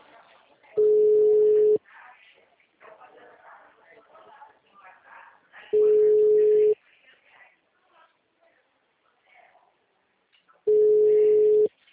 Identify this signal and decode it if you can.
Ringback tone (Europe, 425 Hz), with speech.
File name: out- -2011- - - .0